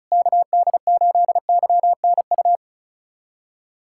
Morse code, heard as a keyed tone, about 34 wpm, 700 Hz. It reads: KD8YNU
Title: USA Callsign DE Callsign - 6 Characters - 35wpm